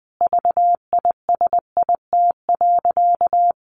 Morse code, heard as a keyed tone, about 20 wpm, 700 Hz. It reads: VISIT.
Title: Sentences from Top 1200 Words 20wpm